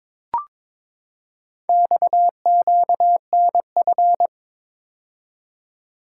Morse code, heard as a keyed tone, about 22 wpm, 700 Hz. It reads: XQNF
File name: practice / 22wpm / Four Letter-Number Combinations - 22wpm